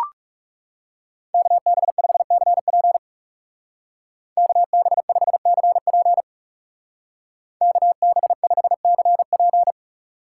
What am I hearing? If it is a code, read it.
Morse code, heard as a keyed tone, about 36 wpm, 700 Hz. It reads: KB5CP KB5CP KB5CP
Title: Speed Racing - USA 2x2 Call Signs 30wpm